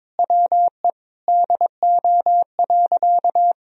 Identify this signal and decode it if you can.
Morse code, heard as a keyed tone, about 22 wpm, 700 Hz. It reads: WE DO.
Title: Sentences from Top 100 Words and Abbreviations 22wpm